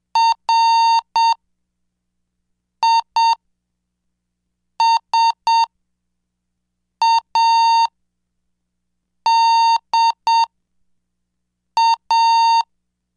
Morse code, heard as a keyed tone, about 7 wpm, spaced as for 4 wpm, 920 Hz. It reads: RISADA
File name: cursos / CDestudo / 11 Track 11